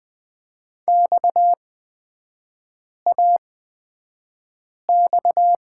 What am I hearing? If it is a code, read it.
Morse code, heard as a keyed tone, about 20 wpm, spaced as for 5 wpm, 700 Hz. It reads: XAX
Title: 229 Lesson 37e - Letter x - Three Characters 20wpm